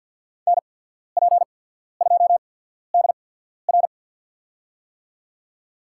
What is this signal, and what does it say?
Morse code, heard as a keyed tone, about 54 wpm, spaced as for 14 wpm, 700 Hz. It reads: NP2DR
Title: USA Call Signs - 5 Characters - 8x Character Spacing - 50wpm